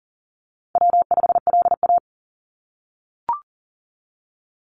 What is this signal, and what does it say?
Morse code, heard as a keyed tone, about 41 wpm, 700 Hz. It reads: W5LA
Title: Call Signs - 40wpm